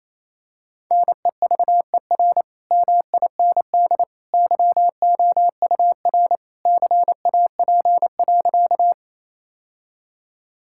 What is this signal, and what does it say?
Morse code, heard as a keyed tone, about 28 wpm, 700 Hz. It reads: NEVER MIND YOUR CAP.
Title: Sentences from Top 1600 Words 28wpm